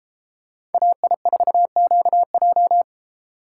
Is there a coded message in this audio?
Yes: Morse code from 0.7 to 2.8 s, heard as a keyed tone, about 31 wpm, 700 Hz. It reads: AI4QJ